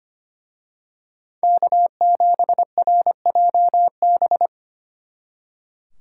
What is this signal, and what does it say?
Morse code, heard as a keyed tone, about 29 wpm, 700 Hz. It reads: K7RJB
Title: Speed Racing - USA 1x3 Call Signs 25wpm